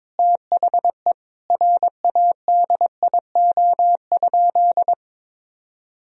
Morse code, heard as a keyed tone, about 21 wpm, 700 Hz. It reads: THE RADIO?
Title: Sentences from Top 1000 Words - Review Part 3 - 22wpm